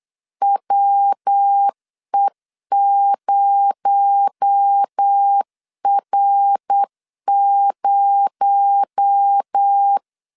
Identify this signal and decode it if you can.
Morse code, heard as a keyed tone, about 8 wpm, 800 Hz. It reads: WE0R0
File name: LEZIONI / 6 PKX0 / 6 PKX0 8wpm 3341